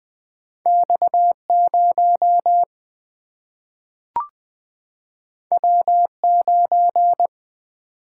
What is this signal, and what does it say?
Morse code, heard as a keyed tone, about 20 wpm, 700 Hz. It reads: X0 W9
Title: Single-Letter Single-Number 20wpm